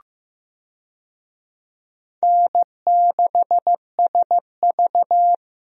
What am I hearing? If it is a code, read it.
Morse code, heard as a keyed tone, about 15 wpm, 700 Hz. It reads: N6SV
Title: USA Call Signs - 4 Characters - Repeated 3x - 15wpm